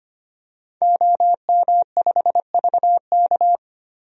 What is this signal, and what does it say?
Morse code, heard as a keyed tone, about 25 wpm, 700 Hz. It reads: OM5VK
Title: DXCC-100 Call Signs - 5 Characters - 25wpm